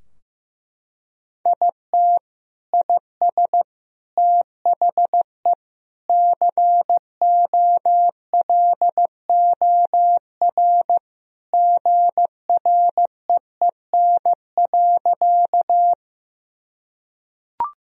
Morse code, heard as a keyed tone, about 16 wpm, 700 Hz. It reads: IT IS THE COLOR GREEN.
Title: Sentences from Top 1000 Words - Review Part 2 - 15wpm